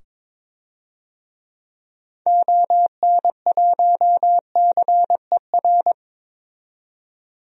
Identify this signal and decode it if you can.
Morse code, heard as a keyed tone, about 22 wpm, 700 Hz. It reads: ON1CER
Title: DXCC-100 Call Signs - 6 Characters - 22wpm